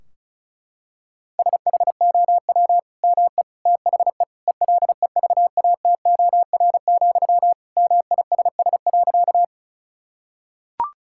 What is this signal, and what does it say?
Morse code, heard as a keyed tone, about 34 wpm, 700 Hz. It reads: SHOW ME THE ELEVATOR, MISS.
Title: Sentences from Top 1600 Words 35wpm